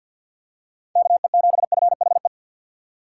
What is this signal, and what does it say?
Morse code, heard as a keyed tone, about 50 wpm, 700 Hz. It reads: KE7FHE
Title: USA Call Signs - 4x Character Spacing - 50wpm